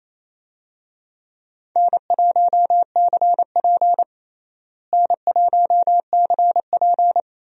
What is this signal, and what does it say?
Morse code, heard as a keyed tone, about 28 wpm, 700 Hz. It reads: N1CP N1CP
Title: USA Call Signs - 4 Characters - Repeated 2x - 28wpm